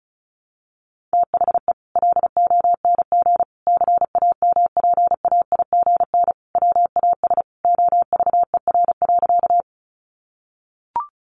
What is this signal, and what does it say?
Morse code, heard as a keyed tone, about 34 wpm, 700 Hz. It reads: THE LONG CAMPAIGN WAS OVER.
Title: Sentences from Top 1200 Words 35wpm